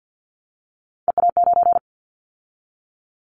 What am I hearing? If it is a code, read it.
Morse code, heard as a keyed tone, about 50 wpm, 700 Hz. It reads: EA9
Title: Callsign Prefixes for Top 100 DXCC - 50wpm